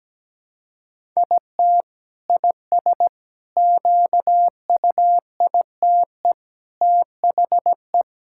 Morse code, heard as a keyed tone, about 17 wpm, 700 Hz. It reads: IT IS QUITE THE
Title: Sentences from Top 1000 Words - Review Part 5 - 17wpm